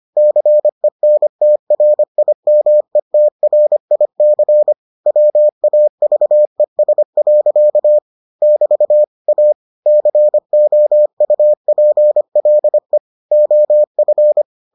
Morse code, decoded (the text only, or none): CENTRIMETRIC WAVES. <BT> A COUPLE OF